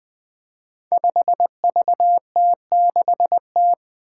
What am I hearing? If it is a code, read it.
Morse code, heard as a keyed tone, about 20 wpm, 700 Hz. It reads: 5VT6T
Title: Five Challenging Letter-Number Combinations 20wpm